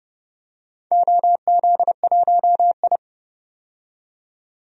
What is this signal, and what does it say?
Morse code, heard as a keyed tone, about 30 wpm, 700 Hz. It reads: OZ1I
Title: DXCC-100 Call Signs - 4 Characters - 30wpm